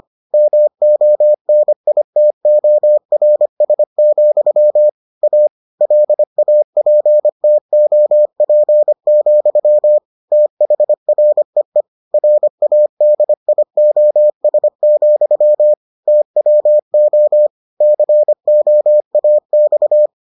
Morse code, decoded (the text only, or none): MONITORS, A LAPTOP, THREE RADIOS, TWO COAX